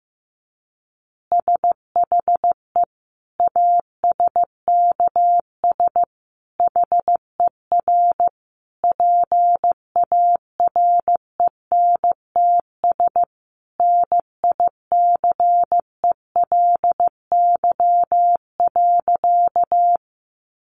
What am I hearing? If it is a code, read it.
Morse code, heard as a keyed tone, about 15 wpm, 700 Hz. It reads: SHE ASKS HER PARENTS NICELY.
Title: Easy Phrases 15wpm